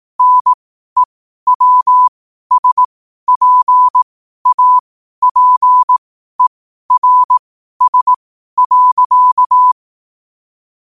Morse code, heard as a keyed tone, about 18 wpm, spaced as for 12 wpm, 1000 Hz. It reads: NEWSPAPERS.
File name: qotdfiles / 12wpm